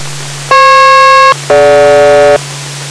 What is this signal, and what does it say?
Aircraft SELCAL, heard as two pairs of tones. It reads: FP-DG